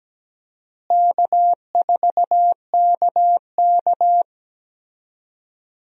Morse code, heard as a keyed tone, about 17 wpm, 700 Hz. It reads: K4KK
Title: USA Call Signs - 4 Characters - Repeated 3x - 17wpm